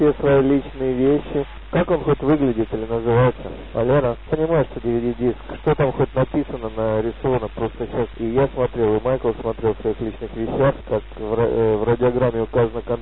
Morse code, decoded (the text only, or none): none